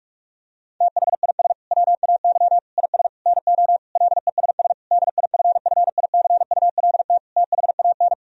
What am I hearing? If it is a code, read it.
Morse code, heard as a keyed tone, about 46 wpm, 700 Hz. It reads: THIS WAY IS NO LESS DIFFICULT THAN
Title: Sentences from Top 1000 Words - Review Part 2 - 45wpm